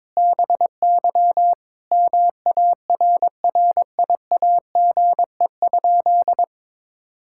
Morse code, heard as a keyed tone, about 22 wpm, 700 Hz. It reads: BY MARRIAGE?
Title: Sentences from Top 1000 Words - Review Part 5 - 22wpm